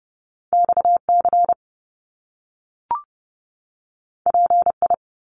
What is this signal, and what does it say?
Morse code, heard as a keyed tone, about 30 wpm, 700 Hz. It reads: XC PI
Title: Top Two Letter Combinations in Words 30wpm